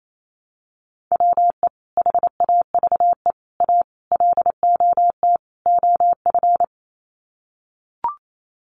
Morse code, heard as a keyed tone, about 28 wpm, 700 Hz. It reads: WE HAVE A LOT OF
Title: Sets of 5 Words 28wpm